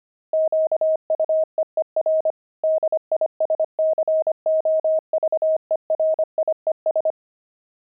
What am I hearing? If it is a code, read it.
Morse code, heard as a keyed tone, about 25 wpm, 630 Hz. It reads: QUEER DISCOVERIES
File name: pangram-fs-25-cs-25-explorer